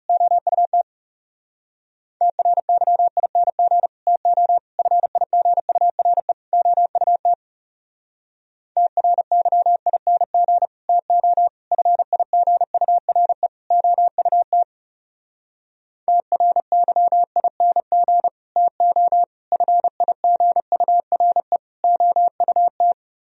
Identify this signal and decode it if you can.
Morse code, heard as a keyed tone, about 34 wpm, 700 Hz. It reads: OUT TRYING TO FIGURE OUT TRYING TO FIGURE OUT TRYING TO FIGURE OUT
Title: Speed-Racing - Sets of 4 Words 30wpm